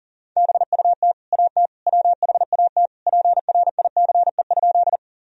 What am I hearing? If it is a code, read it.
Morse code, heard as a keyed tone, about 39 wpm, 700 Hz. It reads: BUT AT WHAT PRICE?
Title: Sentences from Top 1000 Words - Review Part 4 - 40wpm